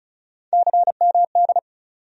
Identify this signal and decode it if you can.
Morse code, heard as a keyed tone, about 36 wpm, 700 Hz. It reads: CMD
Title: USA Call Sign Postfixes - 3 Letters 35wpm